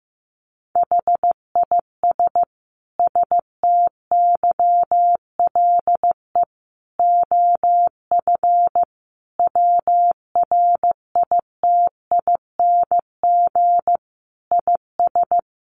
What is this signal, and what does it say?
Morse code, heard as a keyed tone, about 14 wpm, 700 Hz. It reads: HIS STYLE OF WRITING IS
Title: Sentences from Top 800 Words 15wpm